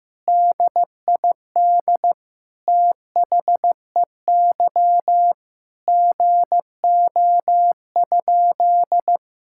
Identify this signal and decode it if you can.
Morse code, heard as a keyed tone, about 16 wpm, 700 Hz. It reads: DID THEY GO?
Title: Sentences from Top 1600 Words 15wpm